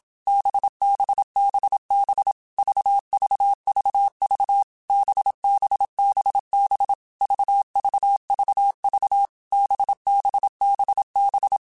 Morse code, heard as a keyed tone, about 27 wpm, 800 Hz. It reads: BBBB VVVV BBBB VVVV BBBB
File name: Basic Letter Groups By7Groups / VB46 / VB46@25